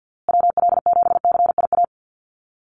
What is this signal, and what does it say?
Morse code, heard as a keyed tone, about 49 wpm, 700 Hz. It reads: WL7CIA
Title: USA Call Signs Repeated 2x 50wpm